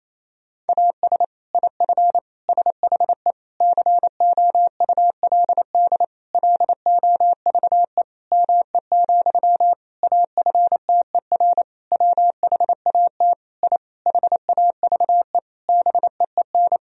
Morse code, heard as a keyed tone, about 28 wpm, 700 Hz. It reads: AS IF SHE COULD LOVE ME, AFTER WHAT I HAVE BEEN
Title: Sentences from Top 1000 Words - Review Part 1 - 28wpm